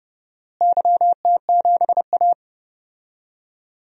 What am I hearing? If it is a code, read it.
Morse code, heard as a keyed tone, about 30 wpm, 700 Hz. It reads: YT7A
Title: DXCC-100 Call Signs - 4 Characters - Repeated 3x - 30wpm